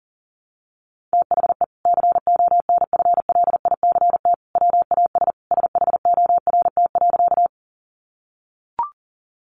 Morse code, heard as a keyed tone, about 40 wpm, 700 Hz. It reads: THE CONFLICT WAS SHORT.
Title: Sentences from Top 1300 Words 40wpm